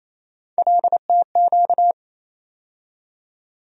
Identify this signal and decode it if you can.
Morse code, heard as a keyed tone, about 28 wpm, 700 Hz. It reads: LTQ
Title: USA Call Sign Postfixes - 3 Letters 28wpm